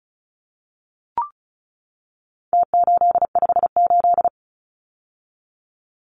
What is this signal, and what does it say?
Morse code, heard as a keyed tone, about 34 wpm, 700 Hz. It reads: T858
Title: Numbers - Contest Style - 4-Digit 35wpm